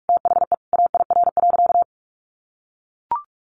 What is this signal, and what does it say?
Morse code, heard as a keyed tone, about 44 wpm, 700 Hz. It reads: THE AIR.
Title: Sentences from Top 900 Words 45wpm